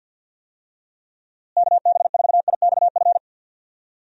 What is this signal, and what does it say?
Morse code, heard as a keyed tone, about 50 wpm, 700 Hz. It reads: KB4IXF
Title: USA Call Signs - 6 Characters - Repeated 3x - 50wpm